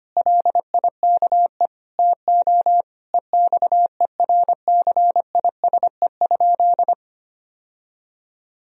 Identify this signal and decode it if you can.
Morse code, heard as a keyed tone, about 25 wpm, 700 Hz. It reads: LIKE TO EXERCISE?
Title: Sentences from Top 1000 Words - Review Part 4 - 25wpm